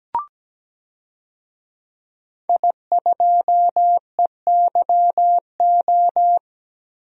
Morse code, heard as a keyed tone, about 17 wpm, 700 Hz. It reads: I2EYO
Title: DXCC-100 Call Signs - 5 Characters - 17wpm